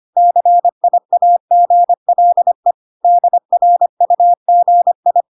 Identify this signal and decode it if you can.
Morse code, heard as a keyed tone, about 25 wpm, 700 Hz. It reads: CIAGLE DRUGI